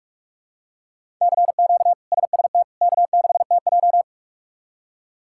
Morse code, heard as a keyed tone, about 46 wpm, 700 Hz. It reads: CQ SST K6TJ
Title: K1USN SST QSO Practice - 45wpm